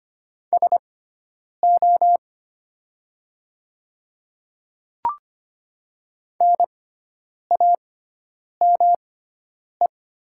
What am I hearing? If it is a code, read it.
Morse code, heard as a keyed tone, about 25 wpm, spaced as for 9 wpm, 700 Hz. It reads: SO NAME
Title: 6x Character Spacing - Top 300 Words - 25wpm